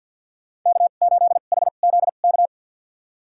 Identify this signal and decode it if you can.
Morse code, heard as a keyed tone, about 50 wpm, spaced as for 36 wpm, 690 Hz. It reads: K8HZK
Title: USA Call Signs - 5 Characters - 2x Character Spacing - 50wpm